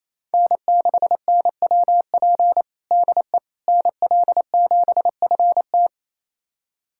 Morse code, heard as a keyed tone, about 28 wpm, 700 Hz. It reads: N6NWP DE NL7FT
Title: USA Callsign DE Callsign - 5 Characters - 28wpm